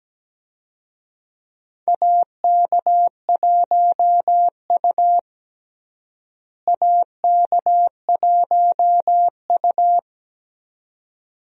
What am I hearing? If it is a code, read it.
Morse code, heard as a keyed tone, about 17 wpm, 700 Hz. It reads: AK1U AK1U
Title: USA Call Signs - 4 Characters - Repeated 2x - 17wpm